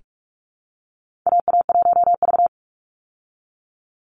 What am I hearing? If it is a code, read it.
Morse code, heard as a keyed tone, about 44 wpm, 700 Hz. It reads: AA1V
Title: Call Signs Repeated 2x - 45wpm